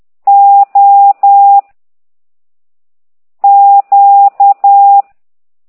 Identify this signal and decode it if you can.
Morse code, heard as a keyed tone, about 10 wpm, 800 Hz. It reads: O Q